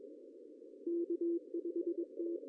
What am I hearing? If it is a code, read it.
Morse code, heard as a keyed tone, about 21 wpm, 340 Hz, with radio noise. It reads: K5T
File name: CW / cqww-cw-k5tr